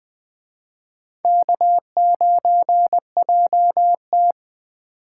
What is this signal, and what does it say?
Morse code, heard as a keyed tone, about 20 wpm, 700 Hz. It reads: K9JT
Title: Speed Racing - USA Call Signs 17wpm